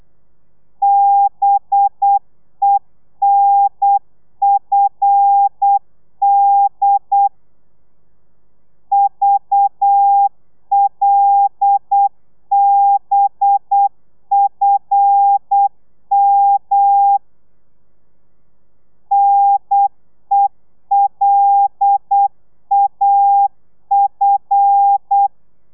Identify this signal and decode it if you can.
Morse code, heard as a keyed tone, about 8 wpm, 800 Hz. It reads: BENFD VLBFM NELAF
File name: telegrafia / i2tfj / Lez5 35